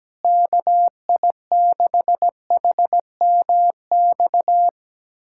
Morse code, heard as a keyed tone, about 17 wpm, 700 Hz. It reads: KI6HMX
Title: USA Callsign DE Callsign 17wpm